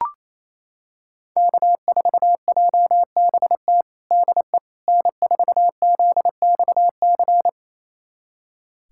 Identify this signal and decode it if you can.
Morse code, heard as a keyed tone, about 28 wpm, 700 Hz. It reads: K4JBT DE N4ZXC
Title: USA Callsign DE Callsign - 5 Characters - 28wpm